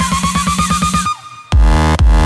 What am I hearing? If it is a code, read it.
There is no code.